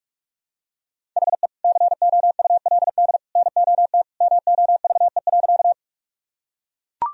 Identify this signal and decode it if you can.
Morse code, heard as a keyed tone, about 43 wpm, 700 Hz. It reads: HE COULD NOT MOVE.
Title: Sentences from Top 1000 Words - Review Part 1 - 45wpm